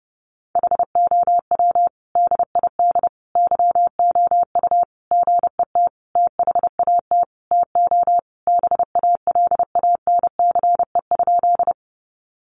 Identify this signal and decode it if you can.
Morse code, heard as a keyed tone, about 30 wpm, 700 Hz. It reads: HOW DID YOU GET THAT TO BALANCE?
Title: Sentences from Top 900 Words 30wpm